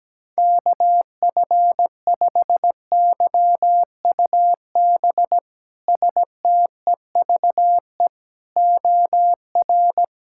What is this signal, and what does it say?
Morse code, heard as a keyed tone, about 17 wpm, 700 Hz. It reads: KF5YUB STEVE OR
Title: K1USN SST QSO Practice - 17wpm